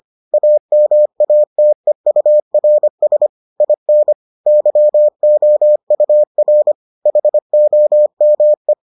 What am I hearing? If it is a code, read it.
Morse code, heard as a keyed tone, about 25 wpm, 600 Hz. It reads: AMATEURS IN YOUR HOME